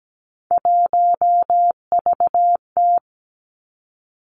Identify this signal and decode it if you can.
Morse code, heard as a keyed tone, about 17 wpm, 700 Hz. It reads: 1VT
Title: Three Challenging Letter-Number Combinations 17wpm